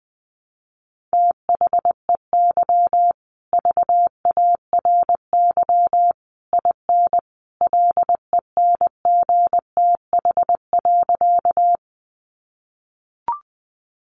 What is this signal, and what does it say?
Morse code, heard as a keyed tone, about 20 wpm, 700 Hz. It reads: THEY VARY IN LENGTH.
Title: Sentences from Top 1200 Words 20wpm